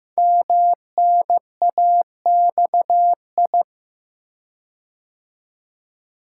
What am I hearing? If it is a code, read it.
Morse code, heard as a keyed tone, about 15 wpm, 700 Hz. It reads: MNAXI
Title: Code Groups - 5 Letters - 15wpm